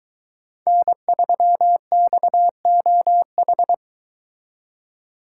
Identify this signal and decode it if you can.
Morse code, heard as a keyed tone, about 22 wpm, 700 Hz. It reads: N3XOH